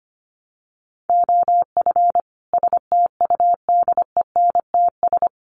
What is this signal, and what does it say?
Morse code, heard as a keyed tone, about 25 wpm, 700 Hz. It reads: OF STUDENTS
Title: Sets of 5 Words Encore 25wpm